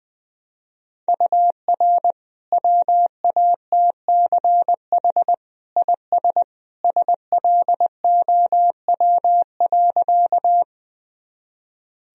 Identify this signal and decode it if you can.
Morse code, heard as a keyed tone, about 20 wpm, 700 Hz. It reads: UR WATCH IS SLOW.
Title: Sentences from Top 400 Words and Abbreviations 20wpm